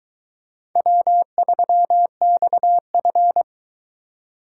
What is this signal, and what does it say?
Morse code, heard as a keyed tone, about 22 wpm, 700 Hz. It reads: W3XF